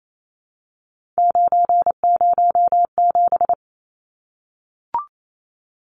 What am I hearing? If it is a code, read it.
Morse code, heard as a keyed tone, about 28 wpm, 700 Hz. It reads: 907
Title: Numbers - 3-Digit 28wpm